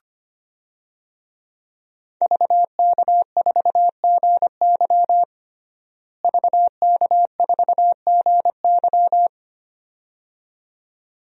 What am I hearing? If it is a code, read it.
Morse code, heard as a keyed tone, about 25 wpm, 700 Hz. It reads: VK4GY VK4GY
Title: DXCC-100 Call Signs - 5 Characters - Repeated 2x - 25wpm